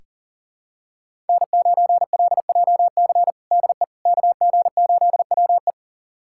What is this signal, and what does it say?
Morse code, heard as a keyed tone, about 40 wpm, 700 Hz. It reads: N9LJC DE KG8WE
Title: USA Callsign DE Callsign - 5 Characters - 40wpm